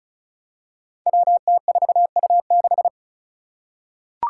Morse code, heard as a keyed tone, about 36 wpm, 700 Hz. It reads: WT4U6
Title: Five Challenging Letter-Number Combinations 35wpm